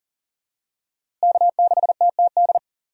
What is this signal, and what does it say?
Morse code, heard as a keyed tone, about 40 wpm, 700 Hz. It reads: K6TTD
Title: USA Call Signs - 5 Characters - Repeated 2x - 40wpm